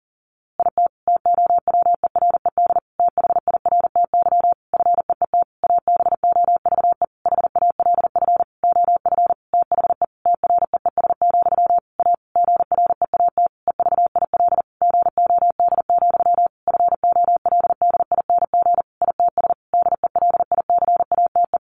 Morse code, heard as a keyed tone, about 37 wpm, 700 Hz. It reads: IT TOWERED THIRTY FEET ABOVE HALF OF THE TREES, A GREAT EVIL GOD, FOLDING ITS DELICATE
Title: A Sound Of Thunder 40wpm